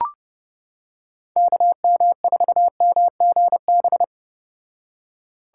Morse code, heard as a keyed tone, about 29 wpm, 700 Hz. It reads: KM4MGB